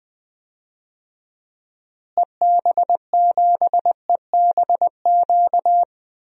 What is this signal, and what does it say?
Morse code, heard as a keyed tone, about 20 wpm, 700 Hz. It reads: EB7EBQ
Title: DXCC-100 Call Signs - 6 Characters - Repeated 3x - 20wpm